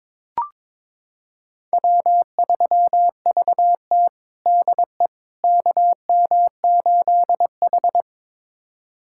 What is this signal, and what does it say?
Morse code, heard as a keyed tone, about 22 wpm, 700 Hz. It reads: W3VT DE KM8H
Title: USA Callsign DE Callsign - 4 Characters - 22wpm